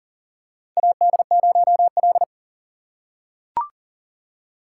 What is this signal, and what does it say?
Morse code, heard as a keyed tone, about 40 wpm, 700 Hz. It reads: AD0L
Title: USA Call Signs - 4 Characters - 3x Character Spacing - 40wpm